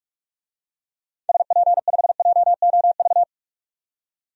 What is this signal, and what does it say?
Morse code, heard as a keyed tone, about 44 wpm, 700 Hz. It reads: SP5JOV
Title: DXCC-100 Call Signs - 6 Characters - 45wpm